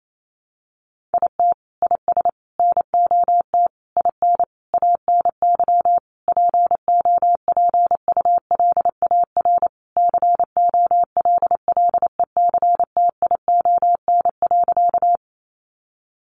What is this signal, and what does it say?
Morse code, heard as a keyed tone, about 28 wpm, 700 Hz. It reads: IT IS NOT IN ANY POPULAR COLLECTION.
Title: Sentences from Top 1200 Words 28wpm